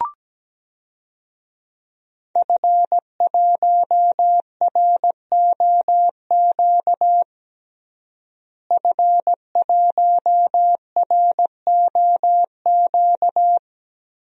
Morse code, heard as a keyed tone, about 17 wpm, 700 Hz. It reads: F1ROQ F1ROQ
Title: DXCC-100 Call Signs - 5 Characters - Repeated 3x - 17wpm